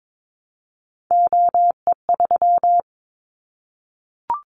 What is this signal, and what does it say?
Morse code, heard as a keyed tone, about 22 wpm, 700 Hz. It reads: OE3
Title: Callsign Prefixes for Top 25 DXCC - 22wpm